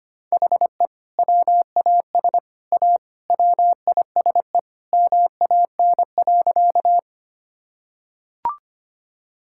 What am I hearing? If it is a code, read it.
Morse code, heard as a keyed tone, about 25 wpm, 700 Hz. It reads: HE WAS A WISE MAN.
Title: Sentences from Top 1600 Words 25wpm